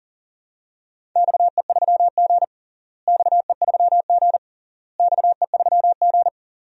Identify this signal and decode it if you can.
Morse code, heard as a keyed tone, about 40 wpm, 700 Hz. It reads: XE3G XE3G XE3G
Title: DXCC-100 Call Signs - 4 Characters - Repeated 3x - 40wpm